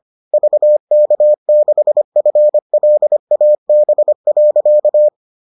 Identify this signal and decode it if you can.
Morse code, heard as a keyed tone, about 25 wpm, 600 Hz. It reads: VK6FLAB.